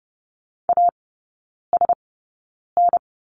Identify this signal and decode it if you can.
Morse code, heard as a keyed tone, about 30 wpm, spaced as for 9 wpm, 700 Hz. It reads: ASN
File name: Lesson 15e - Number 5 - Three Characters 30wpm